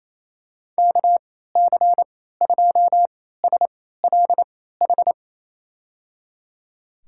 Morse code, heard as a keyed tone, about 28 wpm, spaced as for 16 wpm, 700 Hz. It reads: KC2SLH
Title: USA Call Signs - 6 Characters - 3x Character Spacing - 28wpm